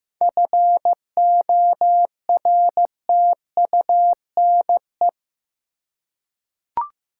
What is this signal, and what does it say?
Morse code, heard as a keyed tone, about 15 wpm, 700 Hz. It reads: FORTUNE
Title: Seven Letter Words - 15wpm